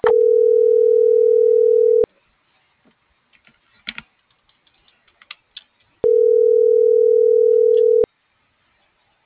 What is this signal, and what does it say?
Ringback tone (North America), 440 + 480 Hz.
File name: CENTRAL GENERAL / out michelle.carrera